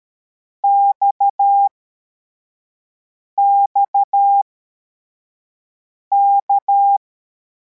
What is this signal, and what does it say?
Morse code, heard as a keyed tone, about 13 wpm, spaced as for 4 wpm, 800 Hz. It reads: XXK
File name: formation / cw / Text23